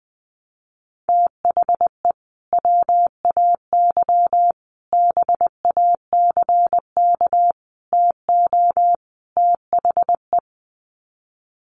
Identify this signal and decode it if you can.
Morse code, heard as a keyed tone, about 19 wpm, 700 Hz. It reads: THE WAY BACK TO THE
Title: Sets of 5 Words Encore2 20wpm